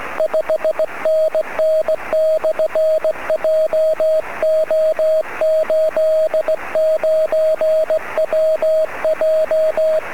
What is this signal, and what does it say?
Morse code, heard as a keyed tone, about 18 wpm, 630 Hz, with radio noise. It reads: 5NN/JO89WJ
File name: sm7lcb / 01-sm0err 23cm